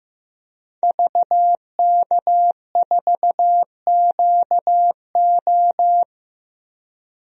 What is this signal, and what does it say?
Morse code, heard as a keyed tone, about 15 wpm, 700 Hz. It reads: VK4QO
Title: DXCC-100 Call Signs - 5 Characters - 15wpm